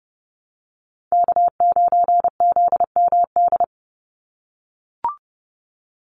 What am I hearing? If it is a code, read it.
Morse code, heard as a keyed tone, about 30 wpm, 700 Hz. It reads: K9ZMD
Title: NA SOTA Chaser Callsigns 30wpm